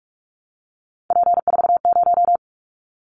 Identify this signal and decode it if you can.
Morse code, heard as a keyed tone, about 46 wpm, 700 Hz. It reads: P40
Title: Callsign Prefixes for Top 100 DXCC - 45wpm